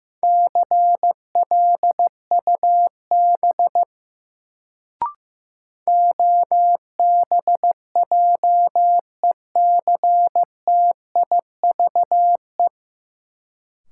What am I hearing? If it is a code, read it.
Morse code, heard as a keyed tone, about 15 wpm, 700 Hz. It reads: CLUB OBJECTIVE
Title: Sentences from Top 1200 Words - New Word Drill Alt - 15wpm